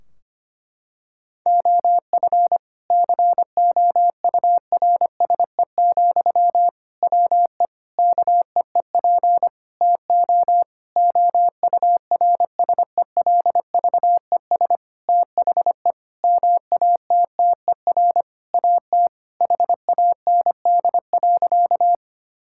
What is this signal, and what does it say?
Morse code, heard as a keyed tone, about 25 wpm, 700 Hz. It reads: OF COURSE, WE KEEP TO OURSELVES THE MATTER AT HAND.
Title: Sentences from Top 1600 Words 25wpm